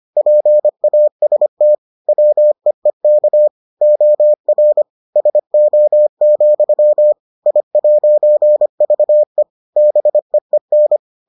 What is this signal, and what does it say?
Morse code, heard as a keyed tone, about 25 wpm, 600 Hz. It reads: PAST WEEK OR SO, I'VE BEEN